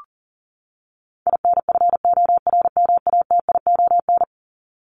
Morse code, heard as a keyed tone, about 40 wpm, 700 Hz. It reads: INFORMATION
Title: Top 501-750 Words 40wpm